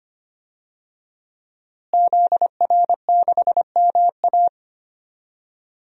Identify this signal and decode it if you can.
Morse code, heard as a keyed tone, about 25 wpm, 700 Hz. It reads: ZR6MA
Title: DXCC-100 Call Signs - 5 Characters - Repeated 3x - 25wpm